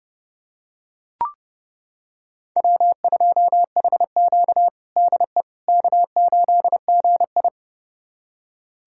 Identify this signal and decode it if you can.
Morse code, heard as a keyed tone, about 30 wpm, 700 Hz. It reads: W2HQ DE K8GI
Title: USA Callsign DE Callsign - 4 Characters - 30wpm